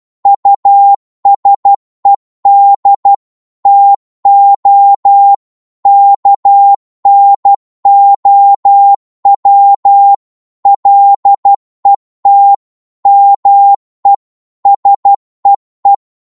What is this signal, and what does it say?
Morse code, heard as a keyed tone, about 12 wpm, 800 Hz. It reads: USED TO KNOW LET ME SEE
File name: 12wpm